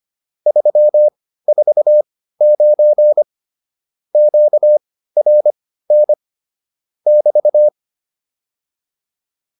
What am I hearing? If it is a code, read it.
Morse code, heard as a keyed tone, about 25 wpm, spaced as for 15 wpm, 600 Hz. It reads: 349 QRN <BT>
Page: intermediate-practice-files